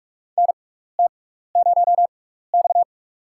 Morse code, heard as a keyed tone, about 45 wpm, spaced as for 15 wpm, 700 Hz. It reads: NT0X
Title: USA Call Signs - 6x Character Spacing - 45wpm